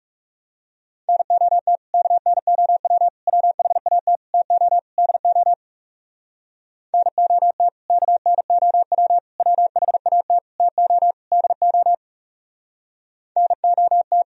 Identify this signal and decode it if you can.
Morse code, heard as a keyed tone, about 37 wpm, 700 Hz. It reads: NOT KNOW WHAT TO DO NOT KNOW WHAT TO DO NOT
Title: Speed-Racing - Sets of 5 Words 30wpm